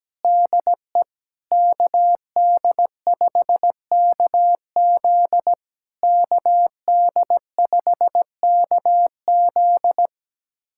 Morse code, heard as a keyed tone, about 17 wpm, 700 Hz. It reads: DE KD5KZ KD5KZ
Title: SOTA QSO Practice 17wpm